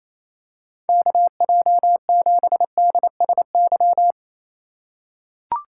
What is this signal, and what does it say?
Morse code, heard as a keyed tone, about 28 wpm, 700 Hz. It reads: KJ7DSY
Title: USA Call Signs - 6 Characters - 2x Character Spacing - 28wpm